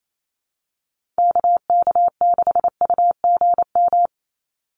Morse code, heard as a keyed tone, about 28 wpm, 700 Hz. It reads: KK6UGM
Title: USA Call Signs 28wpm